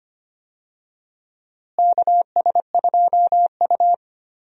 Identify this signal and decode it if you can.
Morse code, heard as a keyed tone, about 25 wpm, 700 Hz. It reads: KS2U
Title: USA Call Signs - 4 Characters - Repeated 2x - 25wpm